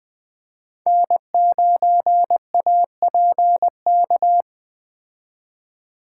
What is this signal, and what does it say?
Morse code, heard as a keyed tone, about 20 wpm, 700 Hz. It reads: N9APK
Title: Speed Racing - USA 1x3 Call Signs 17wpm